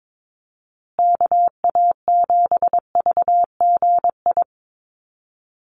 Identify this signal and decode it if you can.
Morse code, heard as a keyed tone, about 22 wpm, 700 Hz. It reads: KA7VGI
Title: USA Call Signs Repeated 2x 22wpm